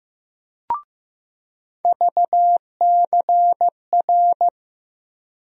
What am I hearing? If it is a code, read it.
Morse code, heard as a keyed tone, about 16 wpm, 700 Hz. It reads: VCR
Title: USA Call Sign Postfixes - 3 Letters 15wpm